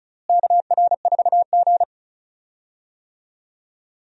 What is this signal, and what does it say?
Morse code, heard as a keyed tone, about 36 wpm, 700 Hz. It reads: KR4G
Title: USA Call Signs - 4 Characters - Repeated 3x - 35wpm